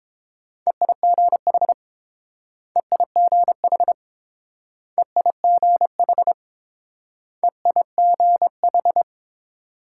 Morse code, heard as a keyed tone, about 29 wpm, 700 Hz. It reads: EIGH EIGH EIGH EIGH